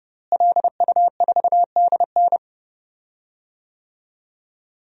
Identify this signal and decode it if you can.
Morse code, heard as a keyed tone, about 32 wpm, 700 Hz. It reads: LU4DN
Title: DXCC-100 Call Signs - 5 Characters - Repeated 3x - 30wpm